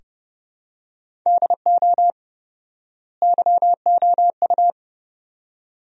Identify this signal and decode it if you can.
Morse code, heard as a keyed tone, about 30 wpm, 700 Hz. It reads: DO YOU
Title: 4x Word-Spacing - Sentences from Top 100 Words - 30wpm